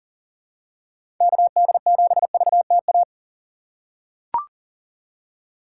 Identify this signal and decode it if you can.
Morse code, heard as a keyed tone, about 40 wpm, 700 Hz. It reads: KD7VTA
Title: USA Call Signs - 6 Characters - 4x Character Spacing - 40wpm